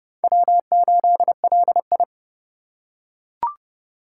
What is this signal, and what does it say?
Morse code, heard as a keyed tone, about 30 wpm, 700 Hz. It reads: W8LI
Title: USA Call Signs - 3x Character Spacing - 30wpm